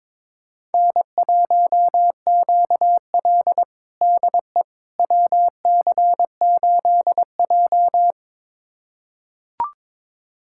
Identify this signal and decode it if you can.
Morse code, heard as a keyed tone, about 22 wpm, 700 Hz. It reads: N1QL DE WC8J
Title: USA Callsign DE Callsign - 4 Characters - 22wpm